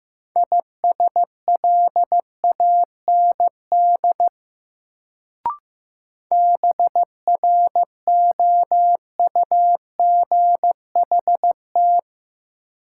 Morse code, heard as a keyed tone, about 15 wpm, 700 Hz. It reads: ISLAND BROUGHT
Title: Sentences from Top 500 Words - New Word Drill - 15wpm